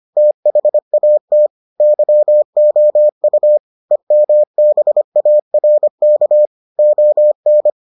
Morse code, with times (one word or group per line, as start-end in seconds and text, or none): 0.2-1.5 THAT
1.8-3.6 YOU
3.9-6.5 EMBARK
6.8-7.7 ON